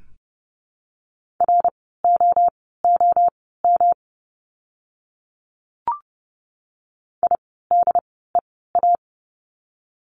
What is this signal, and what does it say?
Morse code, heard as a keyed tone, about 30 wpm, spaced as for 17 wpm, 700 Hz. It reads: ROOM IDEA
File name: course / 30:10wpm / 108 Lesson 18f - Letter m - Words 30wpm